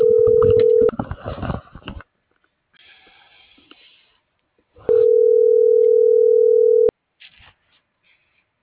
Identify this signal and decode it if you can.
Ringback tone (North America), 440 + 480 Hz.